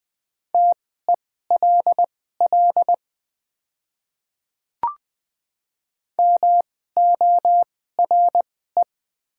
Morse code, heard as a keyed tone, about 20 wpm, spaced as for 14 wpm, 700 Hz. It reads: TELL MORE
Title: 2x Character Spacing - Top 200 Words - 20wpm